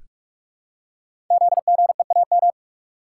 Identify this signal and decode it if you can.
Morse code, heard as a keyed tone, about 44 wpm, 690 Hz. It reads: ZGEAM
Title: Code Groups - 5 Letters - 45wpm